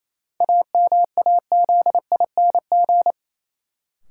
Morse code, heard as a keyed tone, about 28 wpm, 700 Hz. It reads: AMAZING
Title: Sentences from Top 1500 Words - New Word Drill - 28wpm